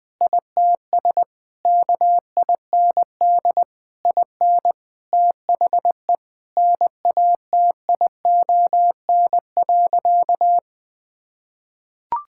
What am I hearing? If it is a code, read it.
Morse code, heard as a keyed tone, about 21 wpm, 700 Hz. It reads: ITS KIND IN THE NATION.
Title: Sentences with 7-Word Phrases - Spoken Beginning - 20wpm